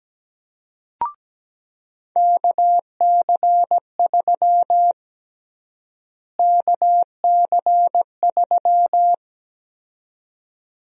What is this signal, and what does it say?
Morse code, heard as a keyed tone, about 17 wpm, 700 Hz. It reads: KC3 KC3
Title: USA Call Sign Prefixes - Repeated 2x - 17wpm